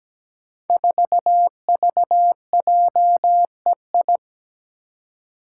Morse code, heard as a keyed tone, about 17 wpm, 700 Hz. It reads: 4VJEI
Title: Five Challenging Letter-Number Combinations 17wpm